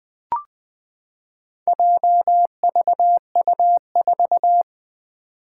Morse code, heard as a keyed tone, about 20 wpm, 700 Hz. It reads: JVU4
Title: Four Challenging Letter-Number Combinations 20wpm